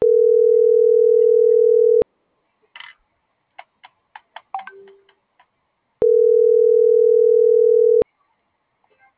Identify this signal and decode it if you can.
Ringback tone (North America), 440 + 480 Hz.